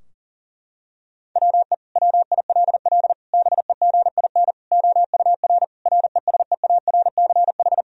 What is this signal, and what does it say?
Morse code, heard as a keyed tone, about 40 wpm, 700 Hz. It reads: WE WILL BEGIN OUR RESEARCH
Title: Sentences from Top 1000 Words - Review Part 4 - 40wpm